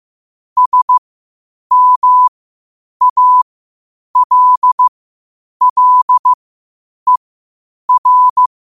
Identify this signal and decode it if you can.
Morse code, heard as a keyed tone, about 15 wpm, spaced as for 8 wpm, 1000 Hz. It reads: SMALLER
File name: qotdfiles / 8wpm